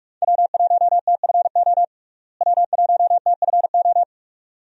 Morse code, heard as a keyed tone, about 46 wpm, 700 Hz. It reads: W1TFO W1TFO
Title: USA Call Signs - 5 Characters - Repeated 2x - 45wpm